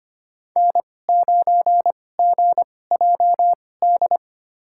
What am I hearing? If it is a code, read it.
Morse code, heard as a keyed tone, about 25 wpm, spaced as for 18 wpm, 700 Hz. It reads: N9GJD
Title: USA Call Signs - 5 Characters - 2x Character Spacing - 25wpm